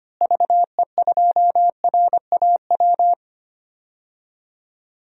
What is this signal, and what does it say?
Morse code, heard as a keyed tone, about 25 wpm, 700 Hz. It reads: VE2RAW
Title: DXCC-100 Call Signs - 6 Characters - Repeated 3x - 25wpm